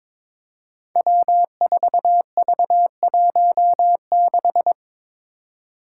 Morse code, heard as a keyed tone, about 22 wpm, 700 Hz. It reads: W4V16
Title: Five Challenging Letter-Number Combinations 22wpm